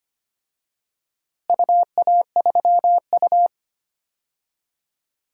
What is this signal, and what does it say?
Morse code, heard as a keyed tone, about 25 wpm, 700 Hz. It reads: UA3U
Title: DXCC-100 Call Signs - 4 Characters - Repeated 2x - 25wpm